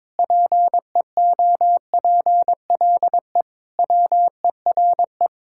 Morse code, heard as a keyed tone, about 22 wpm, 700 Hz. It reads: PEOPLE WERE
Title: Sentences from Top 1000 Words - Review Part 5 - 22wpm